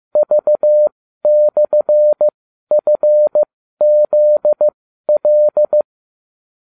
Morse code, heard as a keyed tone, about 15 wpm, spaced as for 12 wpm, 600 Hz. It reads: V/FZL